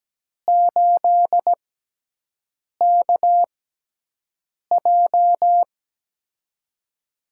Morse code, heard as a keyed tone, about 17 wpm, spaced as for 6 wpm, 700 Hz. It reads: 8KJ
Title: USA Call Signs - 5 Characters - 6x Character Spacing - 17wpm